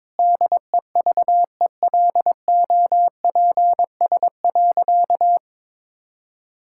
Morse code, heard as a keyed tone, about 22 wpm, 700 Hz. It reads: DEVELOPS.
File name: Sentences with 7-Word Phrases - 22wpm